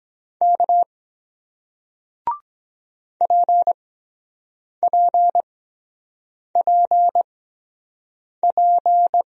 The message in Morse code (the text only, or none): K PPPP